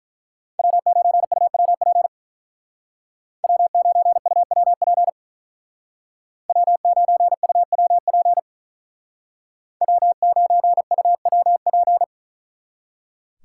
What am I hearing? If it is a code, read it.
Morse code, heard as a keyed tone, about 43 wpm, 700 Hz. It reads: W9UWP W9UWP W9UWP W9UWP
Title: Speed Racing - USA 1x3 Call Signs 35wpm